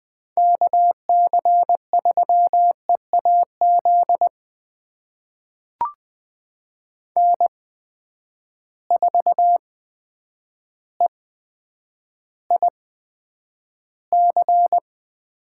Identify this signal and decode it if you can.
Morse code, heard as a keyed tone, about 20 wpm, spaced as for 9 wpm, 700 Hz. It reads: KC3EAZ N4EIC
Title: USA Call Signs - 8x Character Spacing - 20wpm